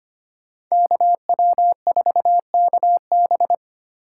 Morse code, heard as a keyed tone, about 33 wpm, 700 Hz. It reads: KW4KB